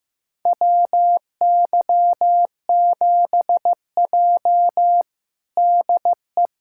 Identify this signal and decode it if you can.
Morse code, heard as a keyed tone, about 15 wpm, 700 Hz. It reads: WY7J DE
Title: USA Callsign DE Callsign - 4 Characters - 15wpm